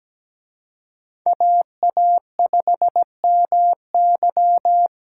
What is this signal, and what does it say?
Morse code, heard as a keyed tone, about 20 wpm, 700 Hz. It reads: AA5MY
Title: Speed Racing - USA 2x2 Call Signs 17wpm